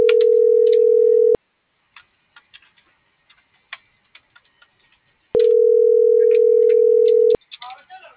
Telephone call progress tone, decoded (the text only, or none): ringback tone